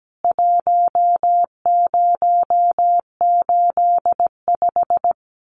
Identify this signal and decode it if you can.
Morse code, heard as a keyed tone, about 17 wpm, 700 Hz. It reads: 1085
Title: CWT Exchange 17wpm